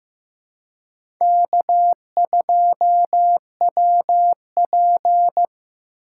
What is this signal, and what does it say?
Morse code, heard as a keyed tone, about 15 wpm, 700 Hz. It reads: K2WP
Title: USA Call Signs - 4 Characters - 3x Character Spacing - 15wpm